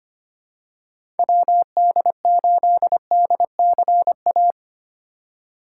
Morse code, heard as a keyed tone, about 25 wpm, 700 Hz. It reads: WD8DCA
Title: USA Call Signs - 3x Character Spacing - 25wpm